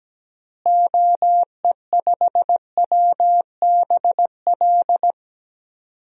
Morse code, heard as a keyed tone, about 17 wpm, 700 Hz. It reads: OE5WBL
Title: DXCC-100 Call Signs - 6 Characters - Repeated 2x - 17wpm